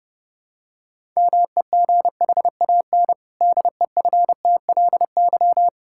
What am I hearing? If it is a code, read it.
Morse code, heard as a keyed tone, about 30 wpm, 700 Hz. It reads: MEGHAN DEFTLY
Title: Pangram Challenge v2 - 30wpm